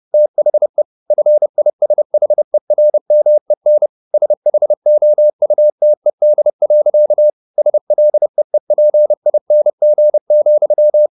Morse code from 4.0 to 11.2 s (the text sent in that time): SHOUTED. SLEEPING,